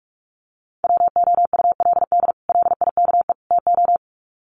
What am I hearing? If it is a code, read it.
Morse code, heard as a keyed tone, about 44 wpm, 700 Hz. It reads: WOULD LIKE TO
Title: Sets of 3 Words 45wpm